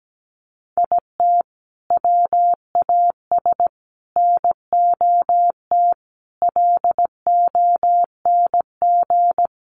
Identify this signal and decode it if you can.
Morse code, heard as a keyed tone, about 17 wpm, 700 Hz. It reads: IT WAS NOT LONG
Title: Sets of 5 Words Encore2 17wpm